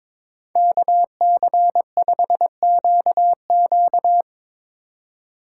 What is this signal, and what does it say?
Morse code, heard as a keyed tone, about 24 wpm, 700 Hz. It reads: KC5QQ
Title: Speed Racing - USA 2x2 Call Signs 22wpm